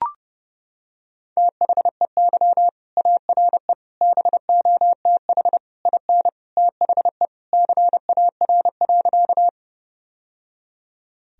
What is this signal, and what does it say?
Morse code, heard as a keyed tone, about 30 wpm, 700 Hz. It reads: THEY ARE BOTH IN THE CAR.
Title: Sentences from Top 1000 Words - Review Part 1 - 30wpm